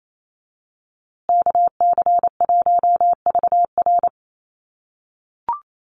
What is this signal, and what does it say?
Morse code, heard as a keyed tone, about 28 wpm, 700 Hz. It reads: KC1VR
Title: USA Call Signs Repeated 3x 28wpm